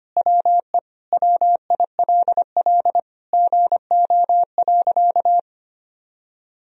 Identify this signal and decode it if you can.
Morse code, heard as a keyed tone, about 25 wpm, 700 Hz. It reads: WE WILL GO.
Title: Sentences from Top 1000 Words - Review Part 1 - 25wpm